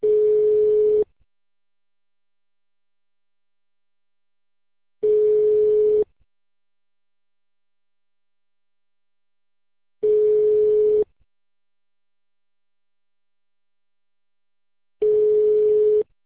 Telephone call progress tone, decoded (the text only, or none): ringback tone